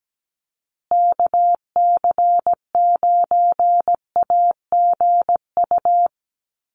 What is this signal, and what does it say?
Morse code, heard as a keyed tone, about 17 wpm, 700 Hz. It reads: KC9AGU
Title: USA Call Signs Repeated 2x 17wpm